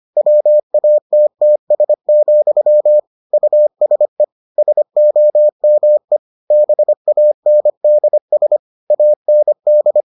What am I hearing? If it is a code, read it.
Morse code, heard as a keyed tone, about 25 wpm, 600 Hz. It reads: WATTS, USE SOME BANDS AND